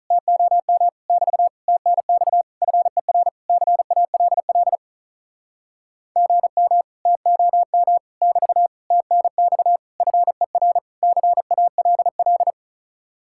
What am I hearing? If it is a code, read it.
Morse code, heard as a keyed tone, about 46 wpm, 700 Hz. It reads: TOM <BT> TNX FER CALL GM TOM <BT> TNX FER CALL